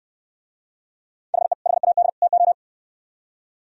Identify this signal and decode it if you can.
Morse code, heard as a keyed tone, about 66 wpm, 700 Hz. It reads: HE HAD TO